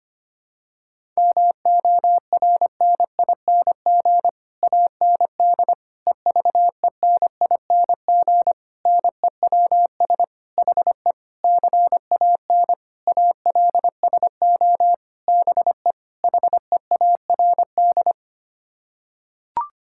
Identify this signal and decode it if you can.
Morse code, heard as a keyed tone, about 25 wpm, 700 Hz. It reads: MORNING AND EVENING NEWS HE CAN ALSO BE HEARD
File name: Sets of 9 Words - 25wpm